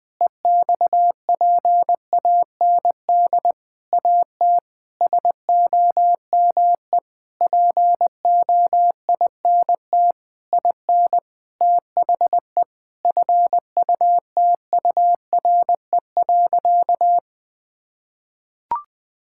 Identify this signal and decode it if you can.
Morse code, heard as a keyed tone, about 20 wpm, 700 Hz. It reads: EXPAND AT SOME POINT IN THE FUTURE.
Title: Sentences with 6-Word Phrases - 20wpm